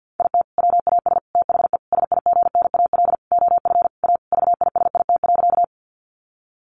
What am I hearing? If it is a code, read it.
Morse code, heard as a keyed tone, about 50 wpm, 700 Hz. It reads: IT WAS THE SIGNAL OF A VISIT.
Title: Sentences from Top 1400 Words 50wpm